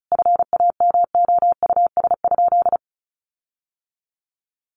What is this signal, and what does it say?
Morse code, heard as a keyed tone, about 34 wpm, 700 Hz. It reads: FAMOUS?
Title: Sentences from Top 1500 Words 35wpm